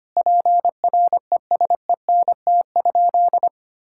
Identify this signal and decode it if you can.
Morse code, heard as a keyed tone, about 25 wpm, 700 Hz. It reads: PRESENT?
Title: Sentences from Top 1000 Words - Review Part 6 - 25wpm